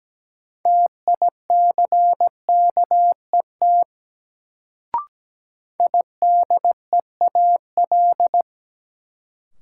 Morse code, heard as a keyed tone, about 17 wpm, 700 Hz. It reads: TICKET IDEAL
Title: Sentences from Top 1400 Words - New Word Drill - 17wpm